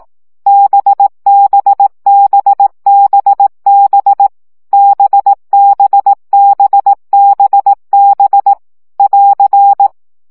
Morse code, heard as a keyed tone, about 18 wpm, 800 Hz. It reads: BBBBB BBBBB <AR>
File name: CW gyakorlas b 15 baud